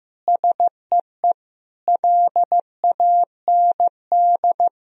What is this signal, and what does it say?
Morse code, heard as a keyed tone, about 15 wpm, 700 Hz. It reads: SEE LAND
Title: Sentences from Top 1000 Words - Review Part 1 - 15wpm